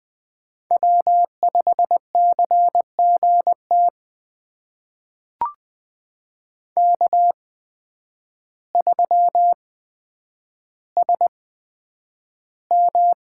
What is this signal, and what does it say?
Morse code, heard as a keyed tone, about 20 wpm, spaced as for 9 wpm, 700 Hz. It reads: W5CGT K3SM
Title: USA Call Signs - 8x Character Spacing - 20wpm